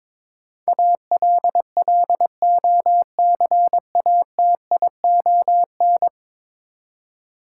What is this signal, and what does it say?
Morse code, heard as a keyed tone, about 22 wpm, 700 Hz. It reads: ALLOCATION
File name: practice / 22wpm / 10 Letter Words - First 3 Letters Given - 22wpm